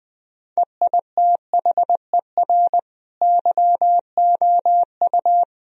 Morse code, heard as a keyed tone, about 21 wpm, 700 Hz. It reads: EITHER YOU
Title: Sentences from Top 1000 Words - Review Part 4 - 20wpm